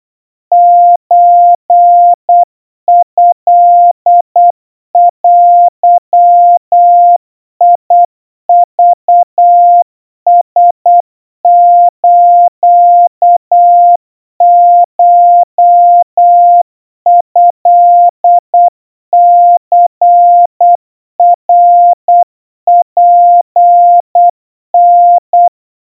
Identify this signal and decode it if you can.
Morse code (Japanese Wabun code), heard as a keyed tone, about 8 wpm, 700 Hz. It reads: ソトデクラスコトニナツタ
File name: 30 cpm 700 Hz Crispy